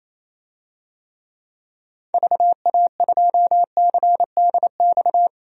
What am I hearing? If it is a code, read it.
Morse code, heard as a keyed tone, about 27 wpm, 700 Hz. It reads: VA2CDX
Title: DXCC-100 Call Signs - 6 Characters - 28wpm